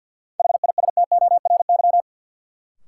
Morse code, heard as a keyed tone, about 52 wpm, 700 Hz. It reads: HISTORY
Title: Sentences from Top 600 Words - New Word Drill - 50wpm